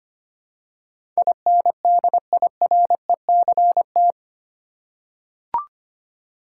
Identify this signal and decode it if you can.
Morse code, heard as a keyed tone, about 25 wpm, 700 Hz. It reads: INDIRECT
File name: 8 Letter Words - First 3 Letters Given - 25wpm